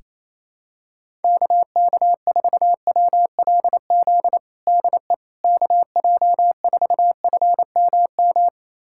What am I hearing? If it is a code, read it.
Morse code, heard as a keyed tone, about 28 wpm, 700 Hz. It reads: KK4WLZ DE KJ4FMM
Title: USA Callsign DE Callsign - 6 Characters - 28wpm